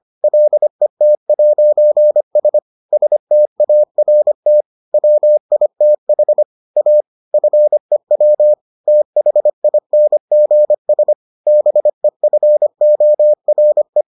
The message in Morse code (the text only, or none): LET'S START WITH A FEW THINGS BEFORE